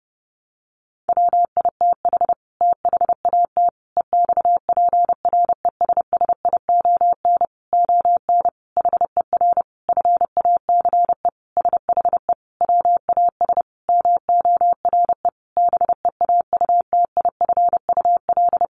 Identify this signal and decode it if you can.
Morse code, heard as a keyed tone, about 30 wpm, 700 Hz. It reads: WITH THAT EXPRESSION ON HER FACE SHE WAS MORE BEAUTIFUL
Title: Sentences from Top 1200 Words 30wpm